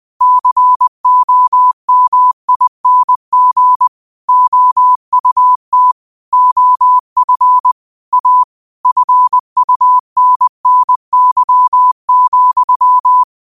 Morse code, heard as a keyed tone, about 20 wpm, 1000 Hz. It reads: COMING OUT OF A FUNNY,